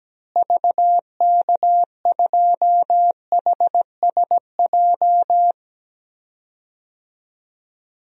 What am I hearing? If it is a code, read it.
Morse code, heard as a keyed tone, about 17 wpm, 700 Hz. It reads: VK2HSJ
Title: DXCC-100 Call Signs - 6 Characters - Repeated 3x - 17wpm